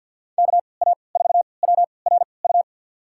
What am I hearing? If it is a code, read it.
Morse code, heard as a keyed tone, about 54 wpm, spaced as for 29 wpm, 700 Hz. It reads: KA4WRU
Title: USA Call Signs - 6 Characters - 3x Character Spacing - 50wpm